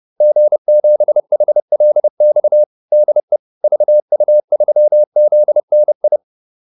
Morse code, heard as a keyed tone, about 30 wpm, 600 Hz. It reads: G7HLX DE VU3ZNI
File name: QSO410 30